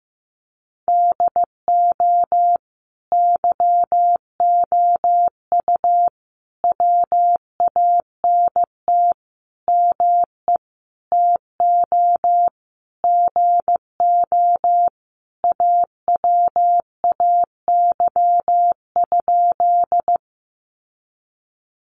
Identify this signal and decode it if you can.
Morse code, heard as a keyed tone, about 14 wpm, 700 Hz. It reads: DO YOU WANT ME TO GO AWAY?
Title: Sentences from Top 600 Words 15wpm